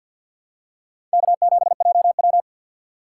Morse code, heard as a keyed tone, about 48 wpm, 700 Hz. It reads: K7JW
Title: USA Call Signs - 4 Characters - 2x Character Spacing - 50wpm